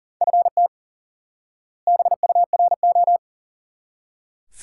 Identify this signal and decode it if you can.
Morse code, heard as a keyed tone, about 40 wpm, 700 Hz. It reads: FT BURO
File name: IQE Course 3 - Lesson 89 - YAGI - Two Elements - 40wpm